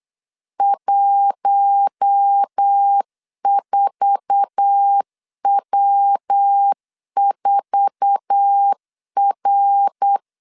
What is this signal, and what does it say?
Morse code, heard as a keyed tone, about 8 wpm, 800 Hz. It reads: 14W4R
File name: LEZIONI / 8 YJ45 / 8 YJ4 8wpm 3246